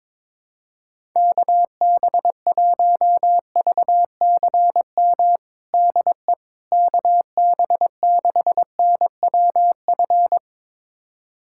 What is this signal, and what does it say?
Morse code, heard as a keyed tone, about 22 wpm, 700 Hz. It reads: KB1VCM DE KB6NWF
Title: USA Callsign DE Callsign - 6 Characters - 22wpm